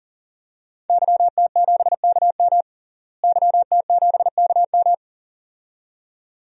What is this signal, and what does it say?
Morse code, heard as a keyed tone, about 40 wpm, 700 Hz. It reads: YT7KM YT7KM
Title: DXCC-100 Call Signs - 5 Characters - Repeated 2x - 40wpm